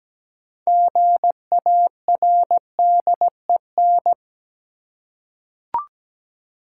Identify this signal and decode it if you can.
Morse code, heard as a keyed tone, about 17 wpm, 700 Hz. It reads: GARDEN
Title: Six Letter Words - 17wpm